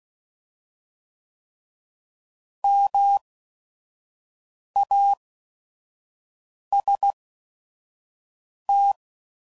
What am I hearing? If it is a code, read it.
Morse code, heard as a keyed tone, about 16 wpm, spaced as for 5 wpm, 800 Hz. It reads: MAST